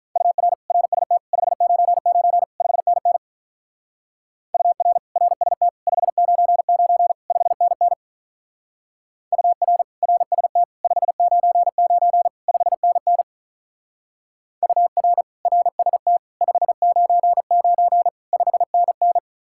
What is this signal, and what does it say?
Morse code, heard as a keyed tone, about 43 wpm, 700 Hz. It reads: UR RST 599 5NN UR RST 599 5NN UR RST 599 5NN UR RST 599 5NN
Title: Speed-Racing Common QSO Phrases - 35wpm